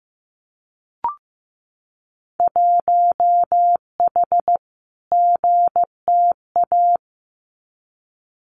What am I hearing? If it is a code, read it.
Morse code, heard as a keyed tone, about 15 wpm, 700 Hz. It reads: 1H GTA
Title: Winter Field Day Exchanges 15wpm